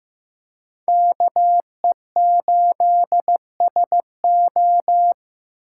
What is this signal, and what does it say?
Morse code, heard as a keyed tone, about 18 wpm, 700 Hz. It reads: KE8SO
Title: Speed Racing - USA 2x2 Call Signs 15wpm